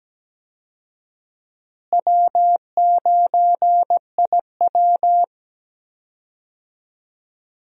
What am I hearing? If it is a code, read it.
Morse code, heard as a keyed tone, about 17 wpm, 700 Hz. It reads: W9IW
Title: USA Call Signs - 4 Characters - Repeated 2x - 17wpm